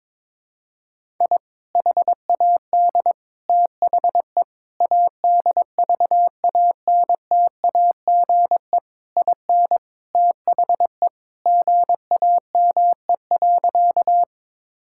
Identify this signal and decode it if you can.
Morse code, heard as a keyed tone, about 22 wpm, 700 Hz. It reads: I HAD THE ADVANTAGE IN THE GAME.
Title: Sentences from Top 1000 Words - Review Part 6 - 22wpm